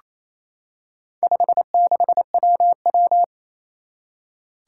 Morse code, heard as a keyed tone, about 27 wpm, 700 Hz. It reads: 56WW
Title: Four Challenging Letter-Number Combinations 28wpm